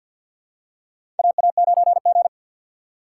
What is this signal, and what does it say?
Morse code, heard as a keyed tone, about 52 wpm, 700 Hz. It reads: AA9G
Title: USA Call Signs - 4 Characters - 3x Character Spacing - 50wpm